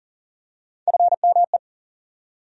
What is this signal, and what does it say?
Morse code, heard as a keyed tone, about 40 wpm, 700 Hz. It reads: FME
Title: USA Call Sign Postfixes - 3 Letters 40wpm